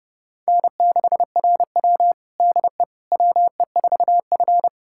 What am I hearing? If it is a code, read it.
Morse code, heard as a keyed tone, about 30 wpm, 700 Hz. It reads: N6RW DE WE4F
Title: USA Callsign DE Callsign - 4 Characters - 30wpm